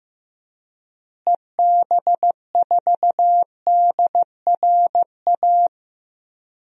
Morse code, heard as a keyed tone, about 15 wpm, 700 Hz. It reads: EB4DRA
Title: DXCC-100 Call Signs - 6 Characters - Repeated 3x - 15wpm